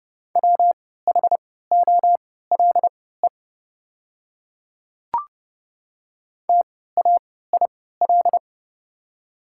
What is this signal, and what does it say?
Morse code, heard as a keyed tone, about 30 wpm, spaced as for 17 wpm, 700 Hz. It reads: WHOLE TAIL
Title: 3x Character Spacing - Top 500 Words - 30wpm